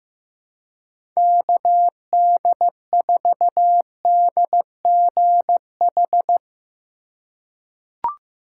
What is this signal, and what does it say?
Morse code, heard as a keyed tone, about 15 wpm, 700 Hz. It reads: KD4DGH
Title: USA Call Signs - 3x Character Spacing - 15wpm